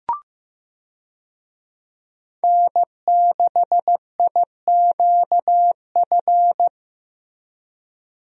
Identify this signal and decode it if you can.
Morse code, heard as a keyed tone, about 15 wpm, 700 Hz. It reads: N6IQF
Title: USA Call Signs - 5 Characters - Repeated 3x - 15wpm